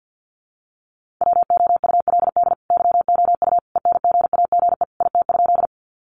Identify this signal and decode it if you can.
Morse code, heard as a keyed tone, about 50 wpm, 700 Hz. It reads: WOULD YOU ENGAGE IT?
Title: Sentences from Top 1500 Words 50wpm